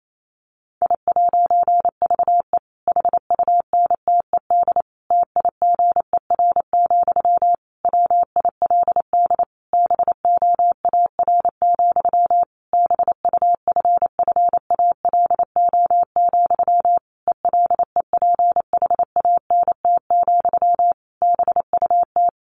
Morse code, heard as a keyed tone, about 28 wpm, 700 Hz. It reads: I'VE HUNTED TIGER, WILD BOAR, BUFFALO, ELEPHANT, BUT
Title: A Sound Of Thunder 28wpm